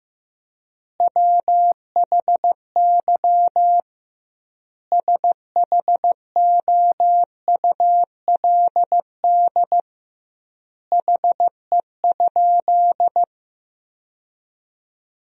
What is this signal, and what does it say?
Morse code, heard as a keyed tone, about 16 wpm, 700 Hz. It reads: WHY SHOULD HE?
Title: 2x Word-Spacing - Sentences from Top 300 Words - 15wpm